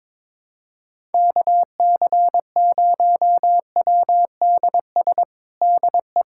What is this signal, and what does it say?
Morse code, heard as a keyed tone, about 22 wpm, 700 Hz. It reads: KC0WDS DE
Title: USA Callsign DE Callsign - 6 Characters - 22wpm